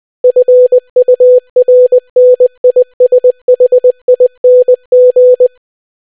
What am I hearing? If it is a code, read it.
Morse code, heard as a keyed tone, about 20 wpm, 500 Hz. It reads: FURNISHING